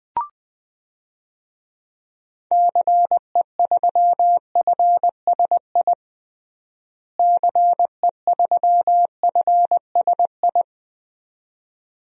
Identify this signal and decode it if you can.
Morse code, heard as a keyed tone, about 20 wpm, 700 Hz. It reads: CE3FSI CE3FSI
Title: DXCC-100 Call Signs - 6 Characters - Repeated 2x - 20wpm